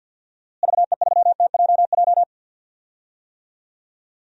Morse code, heard as a keyed tone, about 46 wpm, 700 Hz. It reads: VE3TJJ
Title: DXCC-100 Call Signs - 6 Characters - Repeated 3x - 50wpm